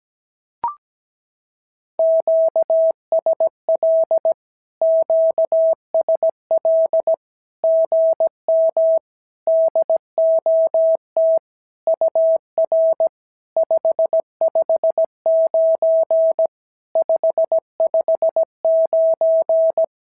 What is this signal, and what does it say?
Morse code, heard as a keyed tone, about 17 wpm, 650 Hz. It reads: QSL QSL GM DOT UR 559 559
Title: SOTA QSO Practice 17wpm